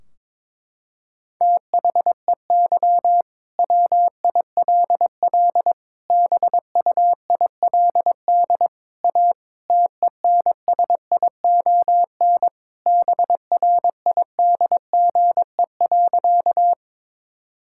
Morse code, heard as a keyed tone, about 22 wpm, 700 Hz. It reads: THEY WILL BUILD A TENSION BRIDGE.
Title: Sentences from Top 1600 Words 22wpm